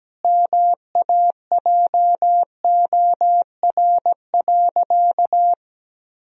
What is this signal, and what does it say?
Morse code, heard as a keyed tone, about 17 wpm, 700 Hz. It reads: MAJOR.
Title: Sentences from Top 1000 Words - Review Part 4 - 17wpm